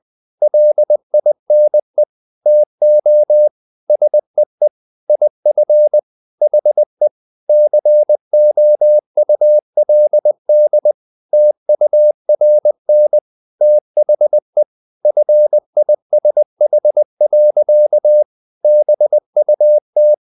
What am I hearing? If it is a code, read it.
Morse code, heard as a keyed tone, about 20 wpm, 600 Hz. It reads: LINE TO SEE IF HE COULD TURN THE FISH. BUT
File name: the old man and the sea / the old man and the sea part 1 20wpm